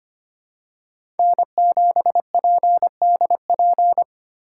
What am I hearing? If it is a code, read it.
Morse code, heard as a keyed tone, about 25 wpm, 700 Hz. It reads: N7PDP
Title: USA Call Signs - 5 Characters - 2x Character Spacing - 25wpm